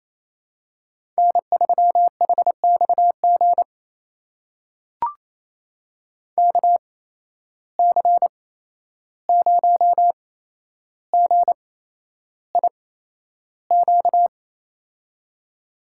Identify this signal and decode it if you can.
Morse code, heard as a keyed tone, about 29 wpm, spaced as for 11 wpm, 700 Hz. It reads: N3HXG KC0GIQ
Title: USA Call Signs - 8x Character Spacing - 28wpm